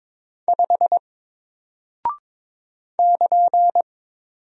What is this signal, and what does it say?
Morse code, heard as a keyed tone, about 22 wpm, 700 Hz. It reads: 5 <KN>
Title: Single Character (Letter:Number:Punctuation:Prosign) Alt 22wpm